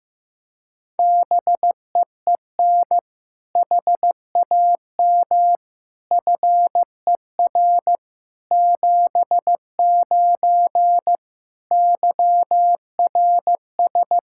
Morse code, heard as a keyed tone, about 15 wpm, 700 Hz. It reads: BEEN HAM FER 79 YRS
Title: Practice QSO - Exchange 3 - More Info - 15wpm